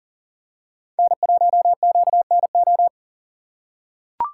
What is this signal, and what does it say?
Morse code, heard as a keyed tone, about 38 wpm, 700 Hz. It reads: N1QNO
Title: USA Call Signs - 5 Characters - 6x Character Spacing - 40wpm